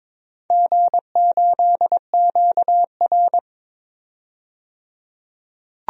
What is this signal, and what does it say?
Morse code, heard as a keyed tone, about 21 wpm, 700 Hz. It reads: G8QR
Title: DXCC-100 Call Signs - 4 Characters - 22wpm